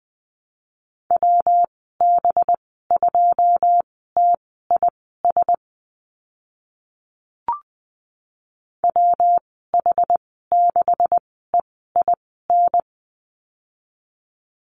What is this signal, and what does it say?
Morse code, heard as a keyed tone, about 20 wpm, spaced as for 14 wpm, 700 Hz. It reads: WB2TIS WH6EIN
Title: 186 Lesson 30g - Letter b - Call Signs 20wpm